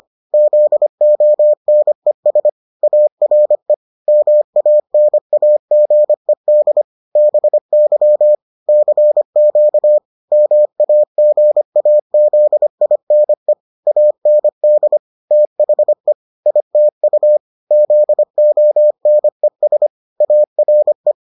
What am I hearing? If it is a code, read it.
Morse code, heard as a keyed tone, about 25 wpm, 600 Hz. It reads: ZONES ARE MANAGED BY CQ MAGAZINE AND THE ITU ZONES ARE